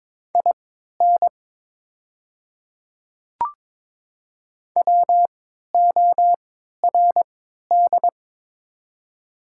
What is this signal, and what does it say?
Morse code, heard as a keyed tone, about 21 wpm, spaced as for 12 wpm, 700 Hz. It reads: IN WORD
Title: 3x Character Spacing - Top 100 Words - 22wpm